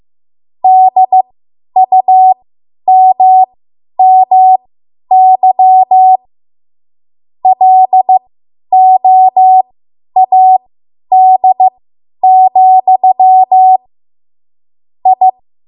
Morse code, heard as a keyed tone, about 15 wpm, spaced as for 10 wpm, 750 Hz. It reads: DUMMY LOAD, I
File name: Morse / 10WPM